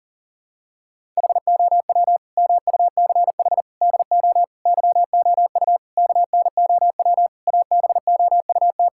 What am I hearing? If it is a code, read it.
Morse code, heard as a keyed tone, about 40 wpm, 700 Hz. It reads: HOW MUCH DO YOU KNOW ABOUT
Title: Sentences from Top 1000 Words - Review Part 6 - 40wpm